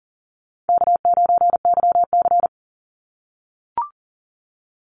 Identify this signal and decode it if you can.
Morse code, heard as a keyed tone, about 40 wpm, 700 Hz. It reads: K9YC
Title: Easy Call Signs - 40wpm